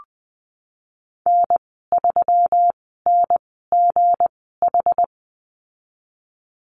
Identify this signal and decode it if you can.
Morse code, heard as a keyed tone, about 20 wpm, spaced as for 15 wpm, 700 Hz. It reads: N3NGH
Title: 167 Lesson 27g - Number 7 - Call Signs 20wpm